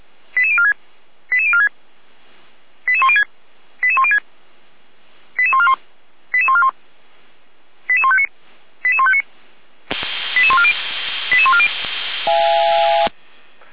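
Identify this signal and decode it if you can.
Five-tone selective call (ZVEI), heard as quick runs of five tones, tones of about 70 ms. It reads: 88046 88046 88106 88106 88141 88141 88159 88159 88140 88140
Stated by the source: ZVEI five-tone call received via SDR, from the audio server of Feuerwehr Orlamünde.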